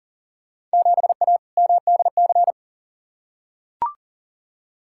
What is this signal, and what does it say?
Morse code, heard as a keyed tone, about 37 wpm, 700 Hz. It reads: 7A MDC
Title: Field Day Exchanges Encore - 40wpm